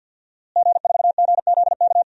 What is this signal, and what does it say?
Morse code, heard as a keyed tone, about 52 wpm, 700 Hz. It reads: G4GZK
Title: DXCC-100 Call Signs - 5 Characters - Repeated 2x - 50wpm